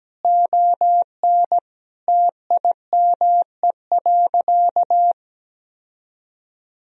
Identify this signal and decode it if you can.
Morse code, heard as a keyed tone, about 17 wpm, 700 Hz. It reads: ON TIME.
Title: Sentences from Top 1000 Words - Review Part 2 - 17wpm